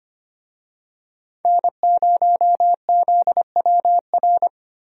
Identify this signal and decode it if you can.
Morse code, heard as a keyed tone, about 25 wpm, 700 Hz. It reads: N0ZWR
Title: USA Call Signs - 5 Characters - Repeated 3x - 25wpm